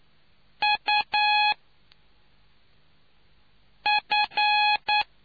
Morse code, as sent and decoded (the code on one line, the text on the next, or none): ..- / ..-.
U F